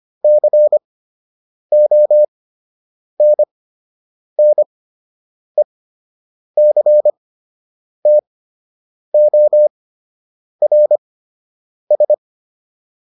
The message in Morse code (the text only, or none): CONNECTORS